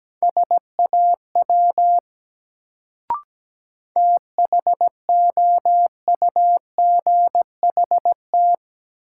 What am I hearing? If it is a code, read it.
Morse code, heard as a keyed tone, about 17 wpm, 700 Hz. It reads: SAW THOUGHT
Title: Sentences from Top 300 Words - New Word Drill - 17wpm